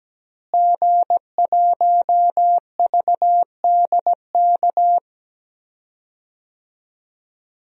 Morse code, heard as a keyed tone, about 17 wpm, 700 Hz. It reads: G1VDK
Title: DXCC-100 Call Signs - 5 Characters - Repeated 3x - 17wpm